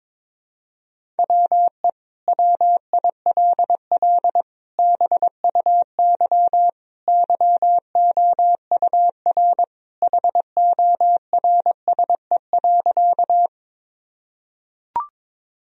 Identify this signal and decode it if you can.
Morse code, heard as a keyed tone, about 22 wpm, 700 Hz. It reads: WE WILL BUY YOUR HORSE.
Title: Sentences from Top 1000 Words - Review Part 4 - 22wpm